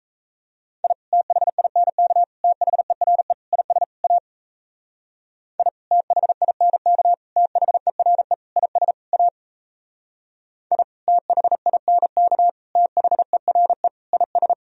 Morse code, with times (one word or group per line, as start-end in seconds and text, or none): 0.8-0.9 I
1.1-2.2 THINK
2.4-3.3 THERE
3.5-3.8 IS
4.0-4.2 A
5.6-5.7 I
5.9-7.1 THINK
7.4-8.3 THERE
8.6-8.9 IS
9.1-9.3 A
10.7-10.8 I
11.1-12.5 THINK
12.8-13.9 THERE
14.1-14.5 IS